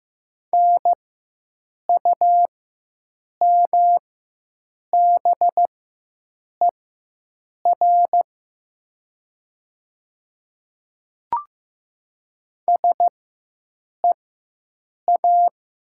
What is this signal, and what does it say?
Morse code, heard as a keyed tone, about 15 wpm, spaced as for 7 wpm, 700 Hz. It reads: NUMBER SEA
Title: 4x Character Spacing - Top 300 Words - 15wpm